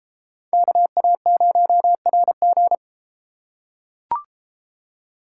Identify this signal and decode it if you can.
Morse code, heard as a keyed tone, about 31 wpm, 700 Hz. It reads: KA0RG